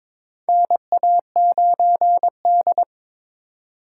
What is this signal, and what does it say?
Morse code, heard as a keyed tone, about 22 wpm, 700 Hz. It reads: NA9D
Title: USA Call Signs - 4 Characters - 4x Character Spacing - 22wpm